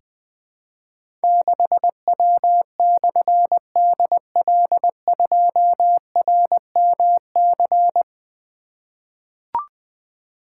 Morse code, heard as a keyed tone, about 20 wpm, 700 Hz. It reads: 6W/DL2RMC
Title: Difficult Call Signs 20wpm